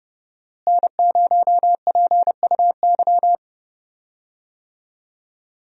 Morse code, heard as a keyed tone, about 31 wpm, 700 Hz. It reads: N0PUY
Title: USA Call Signs - 5 Characters - Repeated 2x - 30wpm